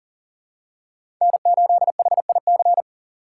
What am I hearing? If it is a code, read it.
Morse code, heard as a keyed tone, about 39 wpm, 700 Hz. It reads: N8HIC
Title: USA Call Signs - 5 Characters - 4x Character Spacing - 40wpm